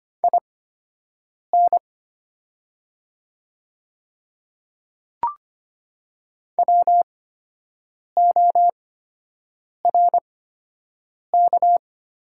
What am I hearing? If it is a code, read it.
Morse code, heard as a keyed tone, about 25 wpm, spaced as for 7 wpm, 700 Hz. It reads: IN WORK
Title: 8x Character Spacing - Top 300 Words - 25wpm